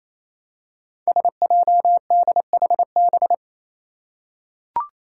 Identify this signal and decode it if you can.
Morse code, heard as a keyed tone, about 28 wpm, 700 Hz. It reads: SJDHB
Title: Five Challenging Letter-Number Combinations 28wpm